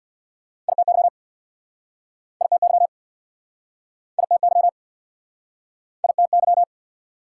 Morse code, heard as a keyed tone, about 69 wpm, 700 Hz. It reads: ITY ITY ITY ITY